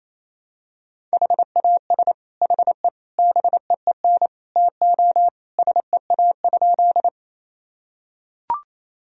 Morse code, heard as a keyed tone, about 28 wpm, 700 Hz. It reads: HAS HE BEEN TO SEA?
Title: Sentences from Top 1000 Words - Review Part 1 - 28wpm